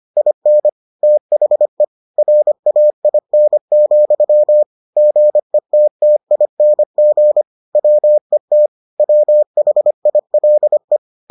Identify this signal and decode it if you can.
Morse code, heard as a keyed tone, about 25 wpm, 600 Hz. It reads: IN THE RAIN, GETTING WET WHILE